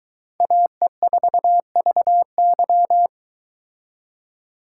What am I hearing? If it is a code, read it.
Morse code, heard as a keyed tone, about 22 wpm, 700 Hz. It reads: AE4VY